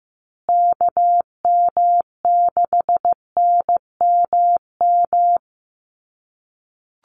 Morse code, heard as a keyed tone, about 15 wpm, 700 Hz. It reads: KM6NMM
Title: USA Call Signs Repeated 2x 15wpm